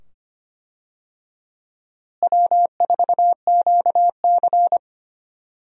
Morse code, heard as a keyed tone, about 25 wpm, 700 Hz. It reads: W4QC
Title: USA Call Signs - 4 Characters - Repeated 2x - 25wpm